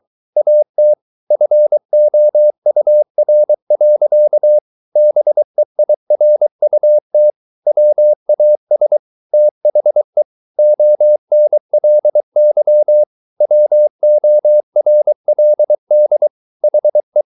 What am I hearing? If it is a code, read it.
Morse code, heard as a keyed tone, about 23 wpm, 600 Hz. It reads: AT FOUR. BEIRUT WAS THE ONLY WORLD HE